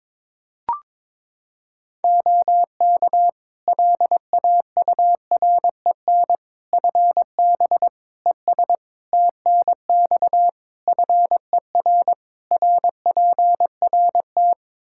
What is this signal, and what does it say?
Morse code, heard as a keyed tone, about 22 wpm, 700 Hz. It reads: OK LAUREN FB ES TNX FER RPRT
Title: Practice QSO - Exchange 2 - Info - 22wpm